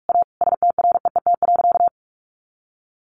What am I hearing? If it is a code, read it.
Morse code, heard as a keyed tone, about 44 wpm, 700 Hz. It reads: A STREET.
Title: Sentences from Top 1100 Words 45wpm